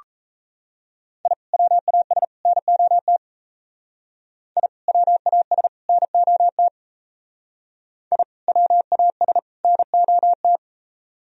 Morse code, heard as a keyed tone, about 35 wpm, 700 Hz. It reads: I WAS NOT I WAS NOT I WAS NOT
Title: Speed-Racing - Sets of 3 Words 28wpm